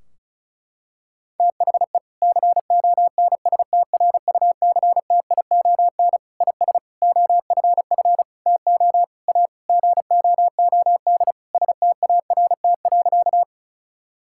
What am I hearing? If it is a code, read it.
Morse code, heard as a keyed tone, about 36 wpm, 700 Hz. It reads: THE CONSTRUCTION IS OFF TO A GOOD START.
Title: Sentences from Top 1600 Words 35wpm